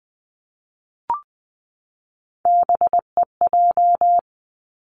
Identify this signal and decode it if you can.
Morse code, heard as a keyed tone, about 20 wpm, 700 Hz. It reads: BEJ
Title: Three Challenging Letter-Number Combinations 20wpm